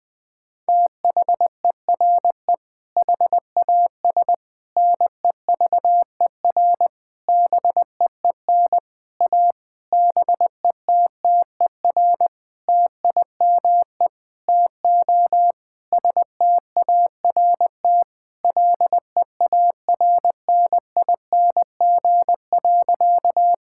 Morse code, heard as a keyed tone, about 20 wpm, 700 Hz. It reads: THERE HAS NEVER BEEN A BETTER TIME TO START LEARNING.
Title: Sentences with 7-Word Phrases - Spoken Beginning - 20wpm